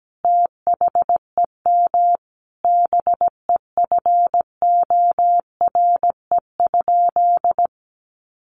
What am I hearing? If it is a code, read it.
Morse code, heard as a keyed tone, about 17 wpm, 700 Hz. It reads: THEM BEFORE?
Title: Sentences from Top 900 Words 17wpm